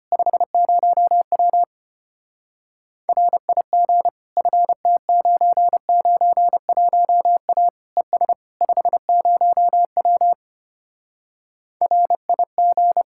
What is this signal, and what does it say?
Morse code, heard as a keyed tone, about 33 wpm, 700 Hz. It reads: 50W RIG FT991A ES 50W RIG